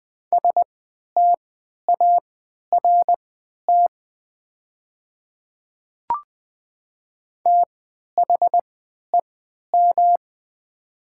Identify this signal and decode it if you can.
Morse code, heard as a keyed tone, about 20 wpm, spaced as for 11 wpm, 700 Hz. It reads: START THEM
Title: 3x Character Spacing - Top 300 Words - 20wpm